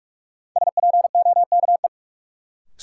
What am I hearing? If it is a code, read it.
Morse code, heard as a keyed tone, about 46 wpm, 700 Hz. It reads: SPOKE
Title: Sentences from Top 1200 Words - New Word Drill - 45wpm